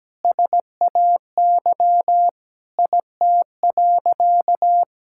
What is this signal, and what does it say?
Morse code, heard as a keyed tone, about 17 wpm, 700 Hz. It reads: SAY IT.
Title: Sentences from Top 1600 Words 17wpm